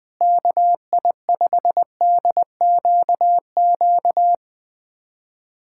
Morse code, heard as a keyed tone, about 21 wpm, 700 Hz. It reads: KI5DQQ
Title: USA Callsign DE Callsign - 6 Characters - 20wpm